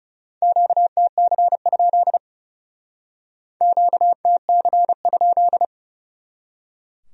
Morse code, heard as a keyed tone, about 37 wpm, 700 Hz. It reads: QTC? QTC?
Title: Speed-Racing Q-Codes - 30wpm